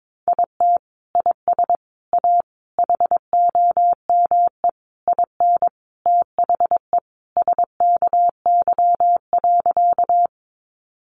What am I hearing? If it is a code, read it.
Morse code, heard as a keyed tone, about 22 wpm, 700 Hz. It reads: IT IS A HOME IN THE SKY.
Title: Sentences from Top 1200 Words 22wpm